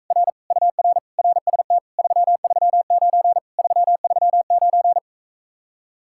Morse code, heard as a keyed tone, about 46 wpm, 700 Hz. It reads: R UR RST 339 339